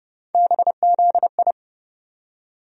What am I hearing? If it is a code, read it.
Morse code, heard as a keyed tone, about 30 wpm, 700 Hz. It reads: BZI
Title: USA Call Sign Postfixes - 3 Letters 30wpm